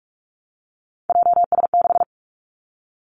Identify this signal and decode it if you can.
Morse code, heard as a keyed tone, about 46 wpm, 700 Hz. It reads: JS6
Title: Three Challenging Letter-Number Combinations 45wpm